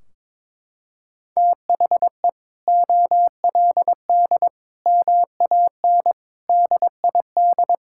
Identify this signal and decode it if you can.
Morse code, heard as a keyed tone, about 22 wpm, 700 Hz. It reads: THE OLD MAN DID
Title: Sentences from Top 1600 Words 22wpm